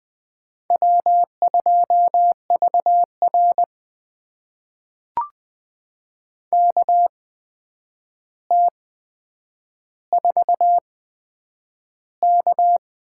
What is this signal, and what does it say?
Morse code, heard as a keyed tone, about 20 wpm, spaced as for 9 wpm, 700 Hz. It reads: W2VR KT4K
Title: USA Call Signs - 4 Characters - 8x Character Spacing - 20wpm